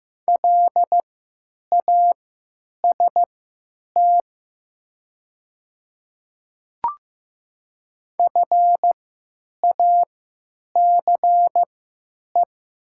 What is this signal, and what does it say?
Morse code, heard as a keyed tone, about 15 wpm, spaced as for 8 wpm, 700 Hz. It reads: LAST FACE
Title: 3x Character Spacing - Top 300 Words - 15wpm